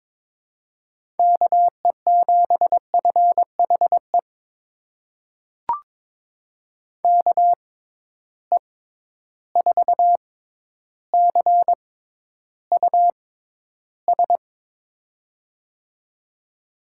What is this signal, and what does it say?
Morse code, heard as a keyed tone, about 22 wpm, spaced as for 11 wpm, 700 Hz. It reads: KE7FHE KE4CUS
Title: USA Call Signs - 6 Characters - 6x Character Spacing - 22wpm